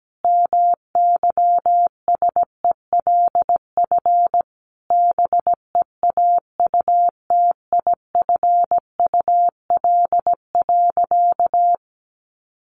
Morse code, heard as a keyed tone, about 17 wpm, 700 Hz. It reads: MYSELF BEAUTIFUL.
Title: Sentences from Top 1300 Words 17wpm